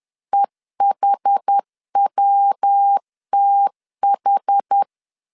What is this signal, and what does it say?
Morse code, heard as a keyed tone, about 10 wpm, 800 Hz. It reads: EHWTH
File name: CW / DET / 1 3 1500C S B 10wpm 3335